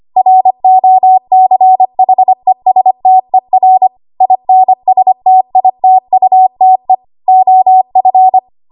Morse code, heard as a keyed tone, about 25 wpm, 750 Hz. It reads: ROCHESTER INSTITUTE OF
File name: Archive / 25 WPM / 25WPM